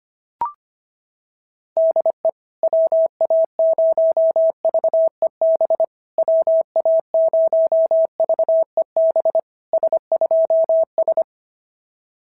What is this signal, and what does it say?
Morse code, heard as a keyed tone, about 25 wpm, 650 Hz. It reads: DE WA0VEB WA0VEB S2S
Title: SOTA QSO Practice 25wpm